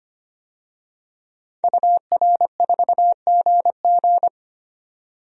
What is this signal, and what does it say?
Morse code, heard as a keyed tone, about 25 wpm, 690 Hz. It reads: UR4GG
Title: DXCC-100 Call Signs - 5 Characters - 25wpm